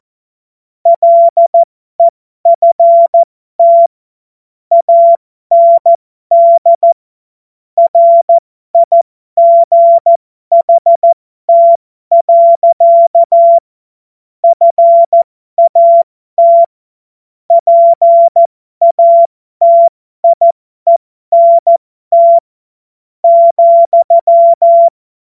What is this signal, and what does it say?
Morse code, heard as a keyed tone, about 14 wpm, spaced as for 12 wpm, 680 Hz. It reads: LEFT AND RIGHT. FAT PATIENT,
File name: Morsekurs / Witze